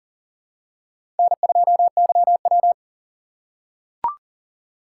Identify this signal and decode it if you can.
Morse code, heard as a keyed tone, about 38 wpm, 700 Hz. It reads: N2YW
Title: USA Call Signs - 6x Character Spacing - 40wpm